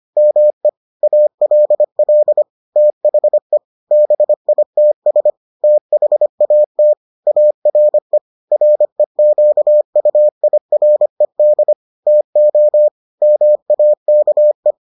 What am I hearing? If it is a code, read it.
Morse code, heard as a keyed tone, about 25 wpm, 600 Hz. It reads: ME ALL THE BITS THAT ARE REQUIRED TO MAKE